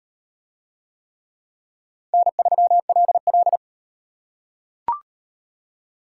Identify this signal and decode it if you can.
Morse code, heard as a keyed tone, about 38 wpm, 700 Hz. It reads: N3LL